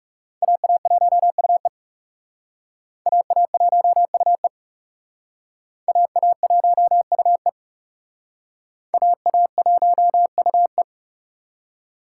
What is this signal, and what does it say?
Morse code, heard as a keyed tone, about 38 wpm, 700 Hz. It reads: AA1UE AA1UE AA1UE AA1UE